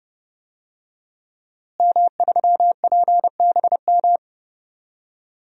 Morse code, heard as a keyed tone, about 32 wpm, 700 Hz. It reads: M3PBM
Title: DXCC-100 Call Signs - 5 Characters - Repeated 2x - 30wpm